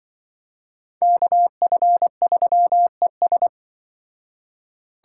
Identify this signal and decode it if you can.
Morse code, heard as a keyed tone, about 24 wpm, 700 Hz. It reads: KF3ES
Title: Speed Racing - USA 2x2 Call Signs 20wpm